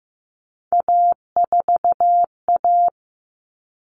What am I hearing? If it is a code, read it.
Morse code, heard as a keyed tone, about 15 wpm, 700 Hz. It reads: A4A
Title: All Cut Numbers - 3-Digit 15wpm